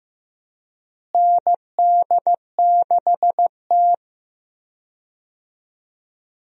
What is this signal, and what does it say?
Morse code, heard as a keyed tone, about 15 wpm, 700 Hz. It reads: ND6T
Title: USA Call Signs - 4 Characters - Repeated 2x - 15wpm